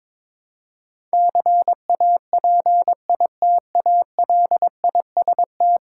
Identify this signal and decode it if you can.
Morse code, heard as a keyed tone, about 22 wpm, 700 Hz. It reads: CAPITALIST
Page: morse-code-ninja-practice-22wpm